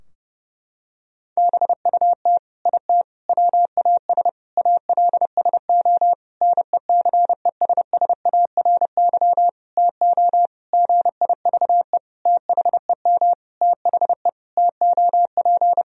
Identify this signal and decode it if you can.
Morse code, heard as a keyed tone, about 30 wpm, 700 Hz. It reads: BUT IT WAS ALSO NECESSARY TO GIVE THEM THE TOP
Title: Sentences from Top 1000 Words - Review Part 2 - 30wpm